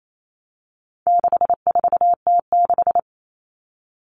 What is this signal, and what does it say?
Morse code, heard as a keyed tone, about 28 wpm, 700 Hz. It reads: 64T6
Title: Common Cut Numbers - 4-Digit 28wpm